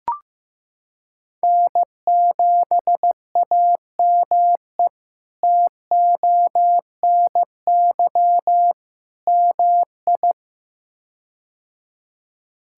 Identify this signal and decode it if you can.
Morse code, heard as a keyed tone, about 15 wpm, 700 Hz. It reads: N7AME TONY MI
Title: K1USN SST QSO Practice - 15wpm